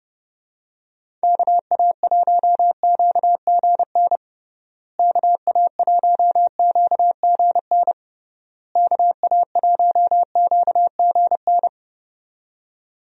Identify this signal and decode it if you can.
Morse code, heard as a keyed tone, about 30 wpm, 700 Hz. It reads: KA1QGN KA1QGN KA1QGN
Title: USA Call Signs - 6 Characters - Repeated 3x - 30wpm